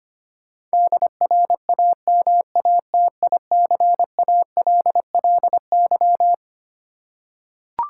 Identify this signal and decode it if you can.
Morse code, heard as a keyed tone, about 25 wpm, 700 Hz. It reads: DRAMATICALLY
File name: practice / 25wpm / Twelve Letter Words - 25wpm